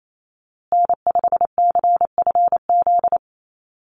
Morse code, heard as a keyed tone, about 28 wpm, 700 Hz. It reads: N5CFZ
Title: USA Call Signs Repeated 2x 28wpm